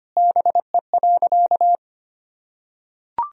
Morse code, heard as a keyed tone, about 25 wpm, 700 Hz. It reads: BE.
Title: Sentences from Top 200 Words and Abbreviations 25wpm